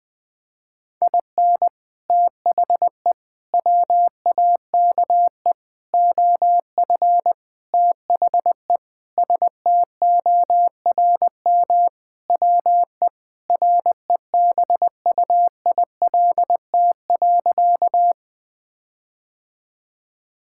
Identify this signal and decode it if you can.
Morse code, heard as a keyed tone, about 21 wpm, 700 Hz. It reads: IN THE WAKE OF THE STORM WE REBUILT.
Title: Sentences with 5-Word Phrases - Spoken Beginning - 20wpm